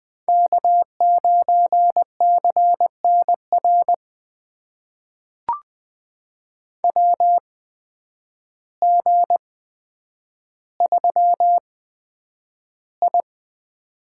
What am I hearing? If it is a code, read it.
Morse code, heard as a keyed tone, about 20 wpm, spaced as for 9 wpm, 700 Hz. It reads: K9CNR WG3I
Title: USA Call Signs - 8x Character Spacing - 20wpm